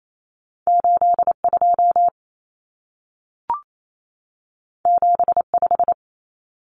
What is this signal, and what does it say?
Morse code, heard as a keyed tone, about 28 wpm, 700 Hz. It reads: 82 75
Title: Numbers - 2-Digit 28wpm